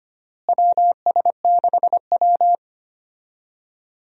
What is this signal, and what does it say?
Morse code, heard as a keyed tone, about 25 wpm, 700 Hz. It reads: WS6W
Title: USA Call Signs - 4 Characters - Repeated 3x - 25wpm